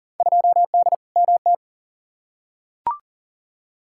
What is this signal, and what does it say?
Morse code, heard as a keyed tone, about 37 wpm, 700 Hz. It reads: 2D MT
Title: Field Day Exchanges Encore - 40wpm